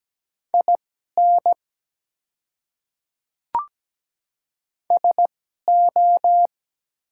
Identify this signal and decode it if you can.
Morse code, heard as a keyed tone, about 17 wpm, spaced as for 12 wpm, 700 Hz. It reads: IN SO
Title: 2x Character Spacing - Top 100 Words - 17wpm